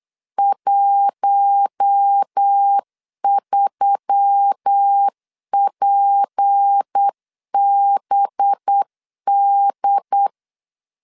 Morse code, heard as a keyed tone, about 8 wpm, 800 Hz. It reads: 13PBD
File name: CW / DET / 1 6 1500C S C 8wpm 5219